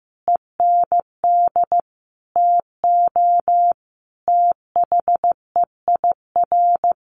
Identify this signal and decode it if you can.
Morse code, heard as a keyed tone, about 15 wpm, 700 Hz. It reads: END TO THEIR
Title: Sentences from Top 1100 Words 15wpm